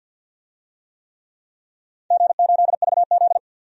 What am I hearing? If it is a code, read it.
Morse code, heard as a keyed tone, about 49 wpm, 700 Hz. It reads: G8VZ
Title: DXCC-100 Call Signs - 4 Characters - 50wpm